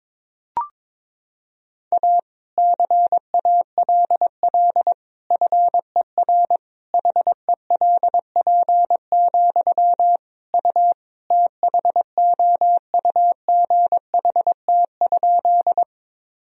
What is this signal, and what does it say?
Morse code, heard as a keyed tone, about 22 wpm, 700 Hz. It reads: A CALL FER HELP, U THOUGHT?
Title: Sentences from Top 300 Words and Abbreviations 22wpm